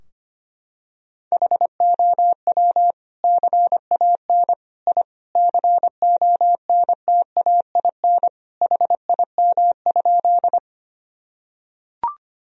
Morse code, heard as a keyed tone, about 25 wpm, 700 Hz. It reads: HOW CAN I CONTAIN HIM?
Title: Sentences from Top 1000 Words - Review Part 2 - 25wpm